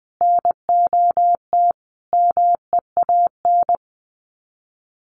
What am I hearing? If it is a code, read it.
Morse code, heard as a keyed tone, about 20 wpm, 700 Hz. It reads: NOT MEAN
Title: Sets of 5 Words Encore2 20wpm